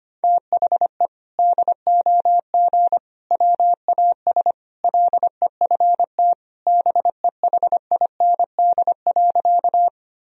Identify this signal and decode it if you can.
Morse code, heard as a keyed tone, about 25 wpm, 700 Hz. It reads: THE DOG WAS LEFT BEHIND.
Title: Sentences from Top 1000 Words - Review Part 2 - 25wpm